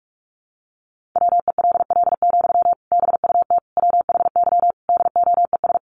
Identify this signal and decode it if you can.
Morse code, heard as a keyed tone, about 44 wpm, 700 Hz. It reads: WELL, BUT WHY DOES
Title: Sentences from Top 700 Words 45wpm